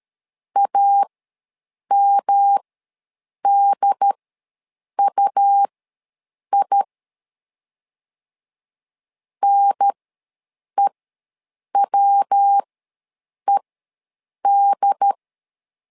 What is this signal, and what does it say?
Morse code, heard as a keyed tone, about 13 wpm, spaced as for 7 wpm, 800 Hz. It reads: AMDUI NEWED
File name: LEZIONI / 3 MWRO / 3 MW ws 12wpm 1816